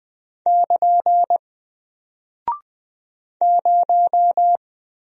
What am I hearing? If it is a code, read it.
Morse code, heard as a keyed tone, about 20 wpm, 700 Hz. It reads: <KN> 0
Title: Single Character (Letter:Number:Punctuation:Prosign) Alt 20wpm